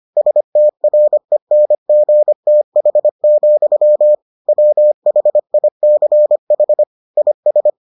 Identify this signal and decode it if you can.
Morse code, heard as a keyed tone, about 25 wpm, 600 Hz. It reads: STRENGTH, WHICH IS